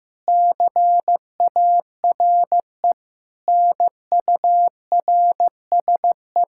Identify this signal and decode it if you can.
Morse code, heard as a keyed tone, about 15 wpm, 700 Hz. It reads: CARE NURSE
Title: K1USN SST QSO Practice 15wpm